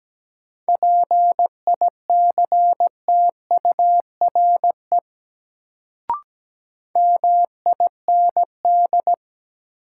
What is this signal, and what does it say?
Morse code, heard as a keyed tone, about 17 wpm, 700 Hz. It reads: PICTURE MIND
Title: Sentences from Top 500 Words - New Word Drill - 17wpm